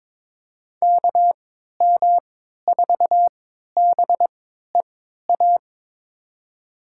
Morse code, heard as a keyed tone, about 22 wpm, spaced as for 13 wpm, 700 Hz. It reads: KM4BEA
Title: USA Call Signs - 3x Character Spacing - 22wpm